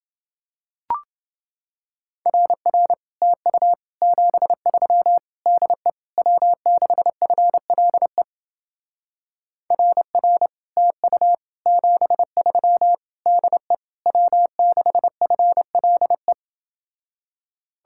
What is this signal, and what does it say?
Morse code, heard as a keyed tone, about 29 wpm, 700 Hz. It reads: RR TU 73 DE W6FLE RR TU 73 DE W6FLE